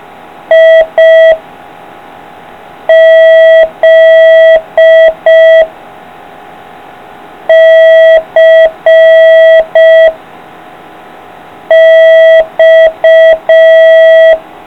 Morse code, heard as a keyed tone, about 5 wpm, 650 Hz. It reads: I Z C X